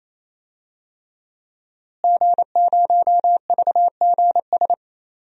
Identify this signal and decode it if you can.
Morse code, heard as a keyed tone, about 27 wpm, 700 Hz. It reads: G0VGS
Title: DXCC-100 Call Signs - 5 Characters - Repeated 3x - 28wpm